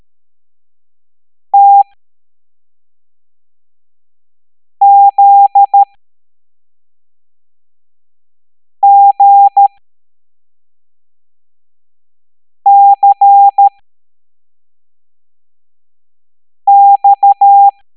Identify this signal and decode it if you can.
Morse code, heard as a keyed tone, about 13 wpm, spaced as for 3 wpm, 800 Hz. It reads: TZGCX